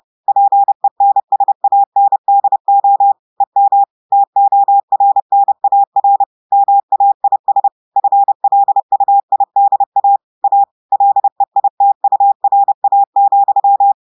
Morse code, heard as a keyed tone, about 30 wpm, 800 Hz. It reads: PENSANDO EM TORNAR MAIS FLUIDA A LEITURA,
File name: audios / portugais / portugais texte en clair 1 vitesse 30 mots minutes